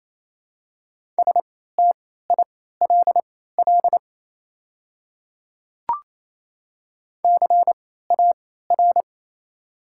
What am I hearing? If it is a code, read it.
Morse code, heard as a keyed tone, about 28 wpm, spaced as for 16 wpm, 700 Hz. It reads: STILL CAR
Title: 3x Character Spacing - Top 300 Words - 28wpm